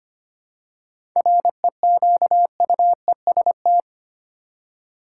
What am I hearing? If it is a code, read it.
Morse code, heard as a keyed tone, about 25 wpm, 700 Hz. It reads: REQUEST
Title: Seven Letter Words - 25wpm